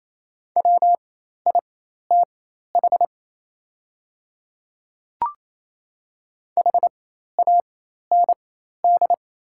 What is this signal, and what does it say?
Morse code, heard as a keyed tone, about 27 wpm, spaced as for 13 wpm, 700 Hz. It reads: WITH HAND
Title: 4x Character Spacing - Top 200 Words - 28wpm